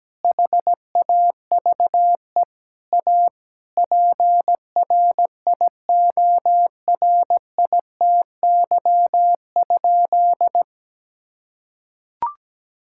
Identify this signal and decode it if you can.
Morse code, heard as a keyed tone, about 17 wpm, 700 Hz. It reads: HAVE A PRIORITY?
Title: Sentences from Top 1600 Words 17wpm